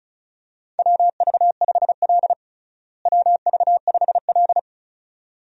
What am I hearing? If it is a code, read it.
Morse code, heard as a keyed tone, about 34 wpm, 700 Hz. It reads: WV5L WV5L
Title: USA Call Signs - 4 Characters - Repeated 2x - 35wpm